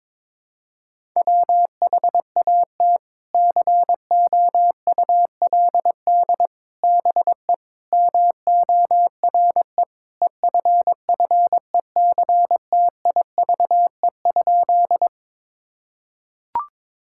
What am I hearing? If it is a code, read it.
Morse code, heard as a keyed tone, about 22 wpm, 700 Hz. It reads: WHAT COULD BE MORE EFFECTIVE?
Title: Sentences from Top 1000 Words - Review Part 6 - 22wpm